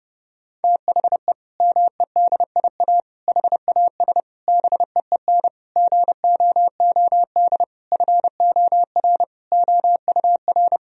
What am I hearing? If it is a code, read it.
Morse code, heard as a keyed tone, about 32 wpm, 700 Hz. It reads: THE MEDIA HAS BEEN GOOD FOR OUR
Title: Sentences from Top 1000 Words - Review Part 4 - 30wpm